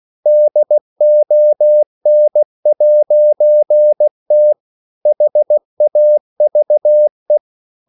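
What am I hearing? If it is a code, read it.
Morse code, heard as a keyed tone, about 16 wpm, 600 Hz. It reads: DON'T HAVE
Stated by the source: English quotations sent in CW as practice at 16 words per minute.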